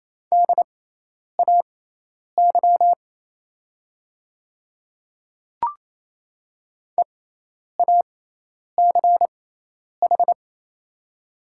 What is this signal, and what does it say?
Morse code, heard as a keyed tone, about 27 wpm, spaced as for 10 wpm, 700 Hz. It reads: DAY EACH
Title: 6x Character Spacing - Top 100 Words - 28wpm